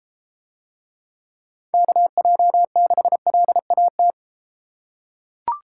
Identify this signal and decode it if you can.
Morse code, heard as a keyed tone, about 31 wpm, 700 Hz. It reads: KJ6LAT